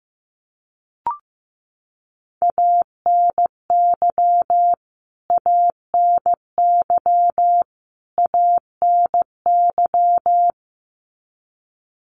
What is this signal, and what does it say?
Morse code, heard as a keyed tone, about 15 wpm, 700 Hz. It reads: ANY ANY ANY
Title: Top 500 Words Repeated 15wpm